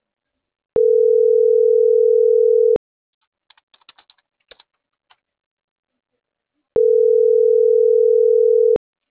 Ringback tone (North America), 440 + 480 Hz.